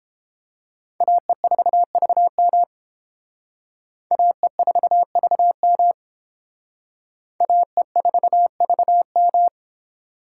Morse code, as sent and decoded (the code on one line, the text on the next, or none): .- . ....- ...- -- / .- . ....- ...- -- / .- . ....- ...- --
AE4VM AE4VM AE4VM